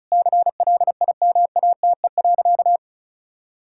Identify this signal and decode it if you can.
Morse code, heard as a keyed tone, about 36 wpm, 700 Hz. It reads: CLIMATE.
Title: Sentences from Top 1600 Words 35wpm